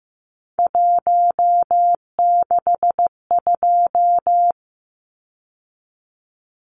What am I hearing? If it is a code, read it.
Morse code, heard as a keyed tone, about 15 wpm, 700 Hz. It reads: 162
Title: Numbers - 3-Digit 15wpm